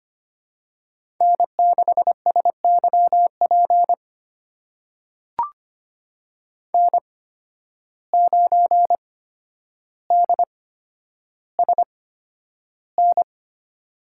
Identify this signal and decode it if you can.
Morse code, heard as a keyed tone, about 25 wpm, spaced as for 11 wpm, 700 Hz. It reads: N6SYP N9DSN
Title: USA Call Signs - 8x Character Spacing - 25wpm